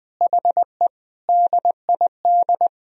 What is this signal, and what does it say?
Morse code, heard as a keyed tone, about 20 wpm, 700 Hz. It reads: HE DID
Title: Sentences from Top 1000 Words - Review Part 1 - 20wpm